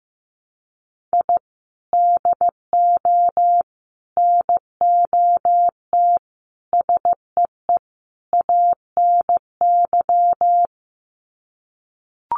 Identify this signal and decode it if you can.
Morse code, heard as a keyed tone, about 15 wpm, 700 Hz. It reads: I DO NOT SEE ANY
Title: Sets of 5 Words 15wpm